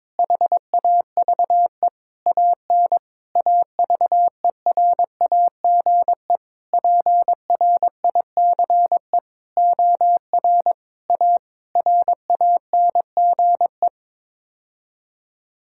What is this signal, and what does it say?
Morse code, heard as a keyed tone, about 22 wpm, 700 Hz. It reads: HAVE AN AVERAGE PRICE OR A RANGE
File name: Sets of 9 Words - 22wpm